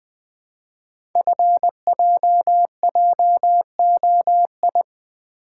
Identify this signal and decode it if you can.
Morse code, heard as a keyed tone, about 20 wpm, 700 Hz. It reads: FJJOI
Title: Code Groups - 5 Letters - 20wpm